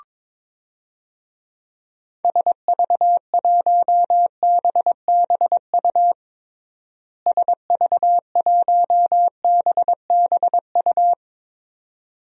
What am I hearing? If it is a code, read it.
Morse code, heard as a keyed tone, about 21 wpm, 700 Hz. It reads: SV1BBU SV1BBU
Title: DXCC-100 Call Signs - 6 Characters - Repeated 2x - 22wpm